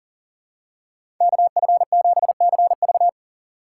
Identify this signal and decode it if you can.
Morse code, heard as a keyed tone, about 40 wpm, 700 Hz. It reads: KF7CV
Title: USA Call Signs - 5 Characters - 2x Character Spacing - 40wpm